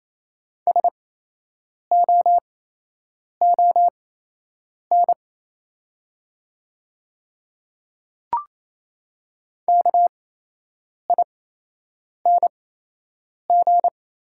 Morse code, heard as a keyed tone, about 28 wpm, spaced as for 8 wpm, 700 Hz. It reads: SOON KING
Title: 8x Character Spacing - Top 500 Words - 28wpm